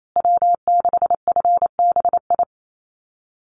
Morse code, heard as a keyed tone, about 28 wpm, 700 Hz. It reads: W6FBI
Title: USA Call Signs Repeated 2x 28wpm